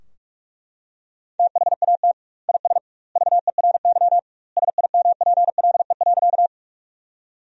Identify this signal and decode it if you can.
Morse code, heard as a keyed tone, about 42 wpm, 700 Hz. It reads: THAT IS VERY SIMPLE.
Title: 2x Word-Spacing - Sentences from Top 400 Words - 45wpm